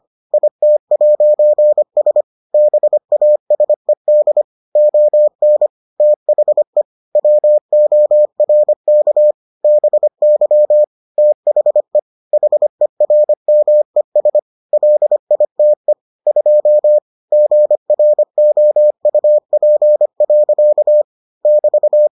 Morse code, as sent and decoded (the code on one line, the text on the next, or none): .. - .----. ... / -... .- ... -.. / --- -. / - .... / .-- --- .-. -.- / -... -.-- / - .... / .... .-. -- . ... / .-.. .. - . / ..--- / --. .-. --- ..- .--. .-.-.- / -...-
IT'S BASED ON THE WORK BY THE HERMES LITE 2 GROUP. <BT>